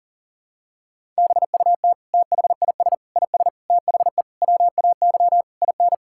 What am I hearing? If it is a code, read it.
Morse code, heard as a keyed tone, about 40 wpm, 700 Hz. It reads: BUT THIS IS THE WAY IN
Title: Sentences from Top 1000 Words - Review Part 5 - 40wpm